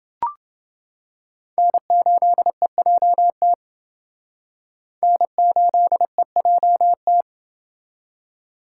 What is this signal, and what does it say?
Morse code, heard as a keyed tone, about 29 wpm, 700 Hz. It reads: N8EJT N8EJT